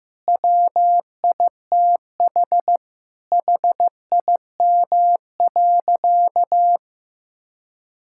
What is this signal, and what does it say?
Morse code, heard as a keyed tone, about 16 wpm, 700 Hz. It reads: WITH HIM.
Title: Sentences from Top 1600 Words 15wpm